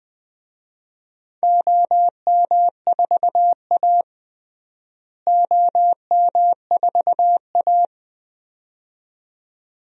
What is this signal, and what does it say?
Morse code, heard as a keyed tone, about 20 wpm, 700 Hz. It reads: OM4A OM4A
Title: DXCC-100 Call Signs - 4 Characters - Repeated 2x - 20wpm